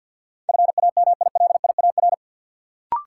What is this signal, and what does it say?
Morse code, heard as a keyed tone, about 53 wpm, 700 Hz. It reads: FAMILIAR
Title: Sentences from Top 1000 Words - New Word Drill - 50wpm